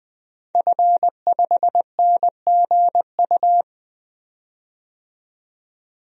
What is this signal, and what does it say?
Morse code, heard as a keyed tone, about 20 wpm, 700 Hz. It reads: F5NGU
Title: DXCC-100 Call Signs - 5 Characters - 20wpm